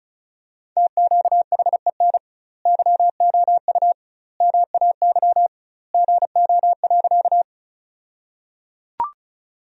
Morse code, heard as a keyed tone, about 36 wpm, 700 Hz. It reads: TQHEN YOU MAY GO.
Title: 2x Word-Spacing - Sentences from Top 100 Words - 35wpm